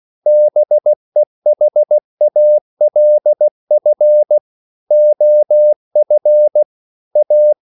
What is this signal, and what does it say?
Morse code, heard as a keyed tone, about 16 wpm, 600 Hz. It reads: BEHALF OF A
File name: CW / Practice / EnglishQuotations0416